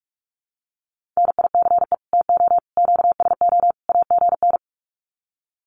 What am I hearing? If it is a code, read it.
Morse code, heard as a keyed tone, about 45 wpm, 700 Hz. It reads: NICE TO QSO AGN
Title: Common QSO Phrases 45wpm